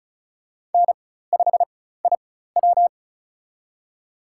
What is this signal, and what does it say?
Morse code, heard as a keyed tone, about 34 wpm, spaced as for 16 wpm, 700 Hz. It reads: N5IW
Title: USA Call Signs - 4 Characters - 4x Character Spacing - 35wpm